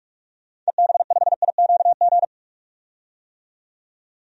Morse code, heard as a keyed tone, about 45 wpm, 700 Hz. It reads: EB5IQG
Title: DXCC-100 Call Signs - 6 Characters - 45wpm